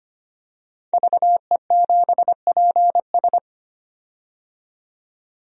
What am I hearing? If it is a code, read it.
Morse code, heard as a keyed tone, about 25 wpm, 700 Hz. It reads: VE7PS
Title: DXCC-100 Call Signs - 5 Characters - 25wpm